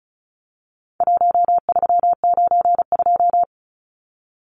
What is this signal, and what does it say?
Morse code, heard as a keyed tone, about 34 wpm, 700 Hz. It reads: 1392
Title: Numbers - Contest Style - 4-Digit 35wpm